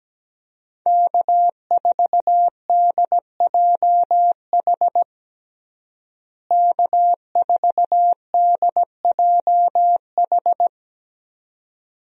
Morse code, heard as a keyed tone, about 17 wpm, 700 Hz. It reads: K4DJH K4DJH
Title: USA Call Signs - 5 Characters - Repeated 2x - 17wpm